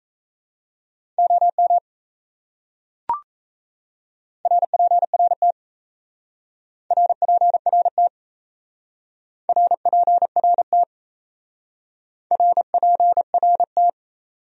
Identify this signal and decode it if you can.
Morse code, heard as a keyed tone, about 34 wpm, 700 Hz. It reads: OM RPRT RPRT RPRT RPRT
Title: Speed-Racing Abbreviations - 28wpm